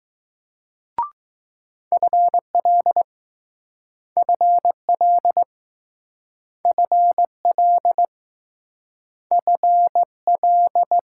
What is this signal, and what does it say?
Morse code, heard as a keyed tone, about 20 wpm, 700 Hz. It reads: FL FL FL FL